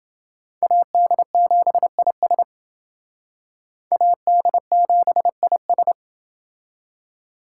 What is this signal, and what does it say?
Morse code, heard as a keyed tone, about 29 wpm, 700 Hz. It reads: AD7IS AD7IS